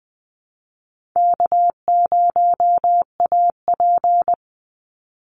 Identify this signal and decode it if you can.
Morse code, heard as a keyed tone, about 20 wpm, 700 Hz. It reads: K0AP
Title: Easy Call Signs Repeated 3x - 20wpm